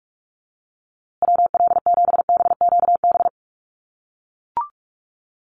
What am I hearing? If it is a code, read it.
Morse code, heard as a keyed tone, about 44 wpm, 690 Hz. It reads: WL7BQB
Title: USA Call Signs Repeated 3x 45wpm